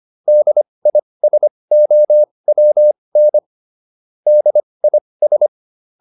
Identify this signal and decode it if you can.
Morse code, heard as a keyed tone, about 25 wpm, spaced as for 20 wpm, 600 Hz. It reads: DISOWN DIS